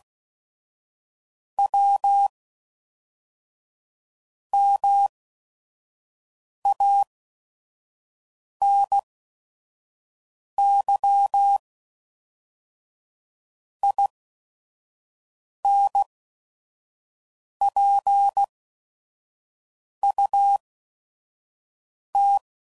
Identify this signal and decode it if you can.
Morse code, heard as a keyed tone, about 16 wpm, spaced as for 5 wpm, 800 Hz. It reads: W MANY INPUT
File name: CW Code Groups @5WPM / Technical Subjects / OR Gate